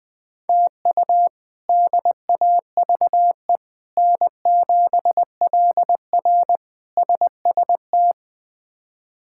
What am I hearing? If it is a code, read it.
Morse code, heard as a keyed tone, about 21 wpm, 700 Hz. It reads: TU DAVE N7LR SST
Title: K1USN SST QSO Practice - 20wpm